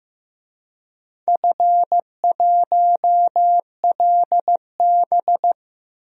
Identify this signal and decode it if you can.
Morse code, heard as a keyed tone, about 15 wpm, 700 Hz. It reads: F1LB
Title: DXCC-100 Call Signs - 4 Characters - Repeated 2x - 15wpm